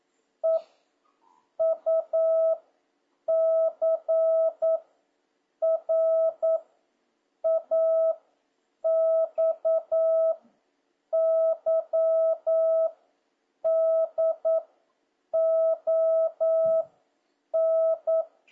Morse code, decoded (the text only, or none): E U C R A X Y D O N